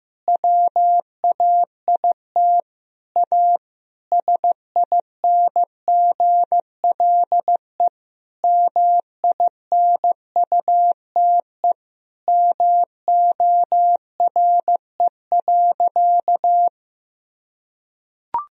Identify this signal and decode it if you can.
Morse code, heard as a keyed tone, about 15 wpm, 700 Hz. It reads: WAIT A SINGLE MINUTE MORE.
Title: Sentences from Top 1000 Words - Review Part 4 - 15wpm